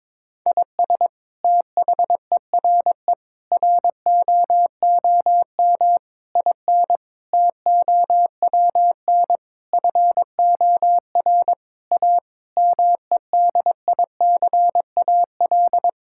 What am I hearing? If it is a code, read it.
Morse code, heard as a keyed tone, about 22 wpm, 700 Hz. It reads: IS THERE ROOM IN TOWN FOR A MEDICAL
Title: Sentences from Top 1000 Words - Review Part 5 - 22wpm